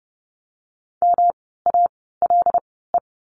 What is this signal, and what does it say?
Morse code, heard as a keyed tone, about 30 wpm, spaced as for 17 wpm, 700 Hz. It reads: MALE
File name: course / 30:10wpm / 108 Lesson 18f - Letter m - Words 30wpm